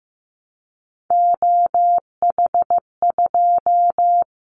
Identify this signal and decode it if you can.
Morse code, heard as a keyed tone, about 15 wpm, 700 Hz. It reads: OH2
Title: Callsign Prefixes for Top 50 DXCC - 15wpm